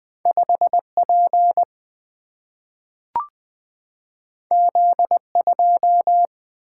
Morse code, heard as a keyed tone, about 20 wpm, 700 Hz. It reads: 5 P Z 2
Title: Single-Letter Single-Number 20wpm